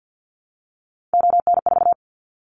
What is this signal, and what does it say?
Morse code, heard as a keyed tone, about 50 wpm, 700 Hz. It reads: ON4
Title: Callsign Prefixes for Top 25 DXCC - 50wpm